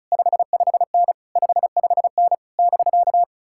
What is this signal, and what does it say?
Morse code, heard as a keyed tone, about 46 wpm, 700 Hz. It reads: 55N 55N <BK>